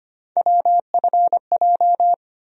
Morse code, heard as a keyed tone, about 25 wpm, 700 Hz. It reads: WFJ
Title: USA Call Sign Postfixes - 3 Letters 25wpm